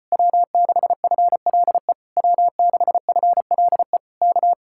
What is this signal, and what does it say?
Morse code, heard as a keyed tone, about 33 wpm, 700 Hz. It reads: W6FLE W6FLE K